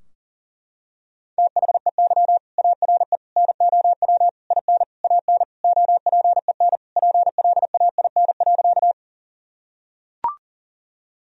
Morse code, heard as a keyed tone, about 40 wpm, 700 Hz. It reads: THEY ARE NOW IN AN OPEN PLAIN.
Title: Sentences from Top 1000 Words - Review Part 2 - 40wpm